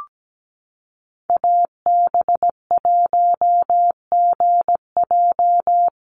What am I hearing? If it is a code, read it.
Morse code, heard as a keyed tone, about 17 wpm, 700 Hz. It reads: AB1GJ
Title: Call Signs Repeated 3x - 17wpm